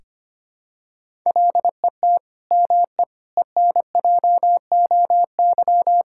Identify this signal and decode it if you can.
Morse code, heard as a keyed tone, about 25 wpm, 700 Hz. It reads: LET ME ENJOY
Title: Sentences from Top 1000 Words - Review Part 5 - 25wpm